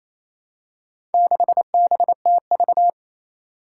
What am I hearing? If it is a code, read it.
Morse code, heard as a keyed tone, about 27 wpm, 700 Hz. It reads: 6BTV
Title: Four Challenging Letter-Number Combinations 28wpm